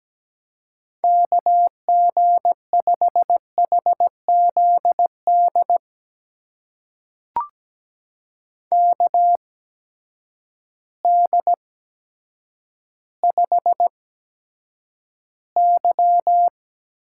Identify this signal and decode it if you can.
Morse code, heard as a keyed tone, about 17 wpm, spaced as for 9 wpm, 700 Hz. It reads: KG5HZD KD5Y
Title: USA Call Signs - 6 Characters - 8x Character Spacing - 17wpm